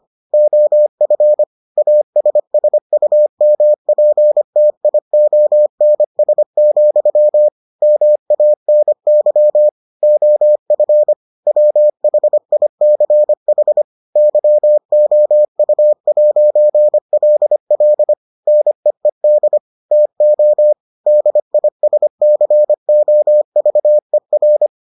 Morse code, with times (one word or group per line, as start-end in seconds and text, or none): 0.3-1.4 OF
1.8-7.5 ASSUMPTIONS,
7.8-9.7 MANY
10.0-11.1 OF
11.5-13.8 WHICH
14.2-18.1 YOU'LL
18.5-19.6 NEED
19.9-20.7 TO
21.1-24.7 DISCOVER